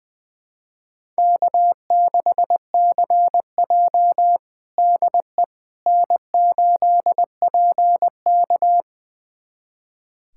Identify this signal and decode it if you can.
Morse code, heard as a keyed tone, about 20 wpm, 700 Hz. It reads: K6CJ DE N8PK
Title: USA Callsign DE Callsign - 4 Characters - 20wpm